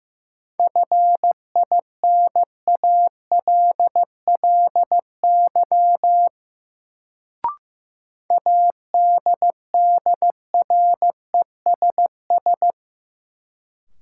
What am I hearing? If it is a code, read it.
Morse code, heard as a keyed tone, about 15 wpm, 700 Hz. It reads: FINALLY ADDRESS
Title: Sentences from Top 1000 Words - New Word Drill - 15wpm